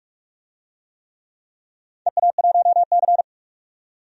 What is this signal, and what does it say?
Morse code, heard as a keyed tone, about 41 wpm, 700 Hz. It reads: EA1C
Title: DXCC-100 Call Signs - 4 Characters - 45wpm